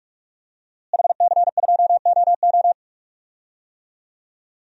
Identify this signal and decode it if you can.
Morse code, heard as a keyed tone, about 46 wpm, 700 Hz. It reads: HC2OO
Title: DXCC-100 Call Signs - 5 Characters - Repeated 2x - 45wpm